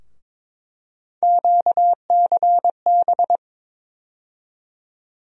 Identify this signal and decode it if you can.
Morse code, heard as a keyed tone, about 22 wpm, 700 Hz. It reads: QCB
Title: USA Call Sign Postfixes - 3 Letters 22wpm